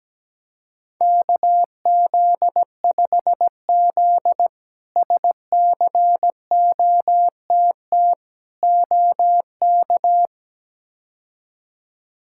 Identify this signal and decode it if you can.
Morse code, heard as a keyed tone, about 17 wpm, 700 Hz. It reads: KZ5Z SCOTT OK
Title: K1USN SST QSO Practice - 17wpm